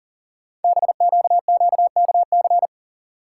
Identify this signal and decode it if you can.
Morse code, heard as a keyed tone, about 40 wpm, 700 Hz. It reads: BQQKC
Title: Code Groups - 5 Letters - 40wpm